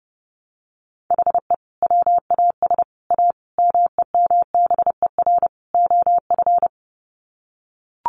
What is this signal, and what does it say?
Morse code, heard as a keyed tone, about 28 wpm, 700 Hz. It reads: HE WAS A MEMBER OF
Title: Sets of 5 Words Encore2 30wpm